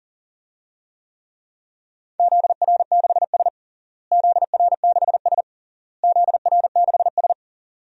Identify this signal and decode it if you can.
Morse code, heard as a keyed tone, about 37 wpm, 700 Hz. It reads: ZR6S ZR6S ZR6S
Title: DXCC-100 Call Signs - 4 Characters - Repeated 3x - 40wpm